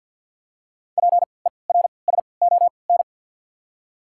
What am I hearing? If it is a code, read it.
Morse code, heard as a keyed tone, about 53 wpm, spaced as for 29 wpm, 700 Hz. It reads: PERSON
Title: 3x Character Spacing - Top 500 Words - 50wpm